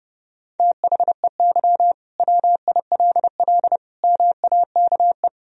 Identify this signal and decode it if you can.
Morse code, heard as a keyed tone, about 30 wpm, 700 Hz. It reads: THEY WILL MAKE
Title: Sentences from Top 1000 Words - Review Part 3 - 30wpm